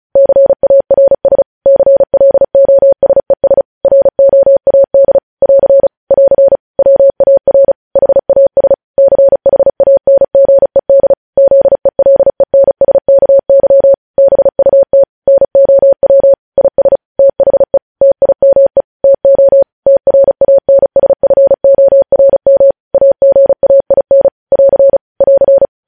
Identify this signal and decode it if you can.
Morse code, heard as a keyed tone, about 36 wpm, 550 Hz. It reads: CARS CLOSES ROAD <AR> <AR> WAR HAS CHANGED ZELENSKY BUT NOW IS THE TIME TO TRANSFORM AGAIN <AR> <AR>
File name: podcast / News headlines-1x-35x35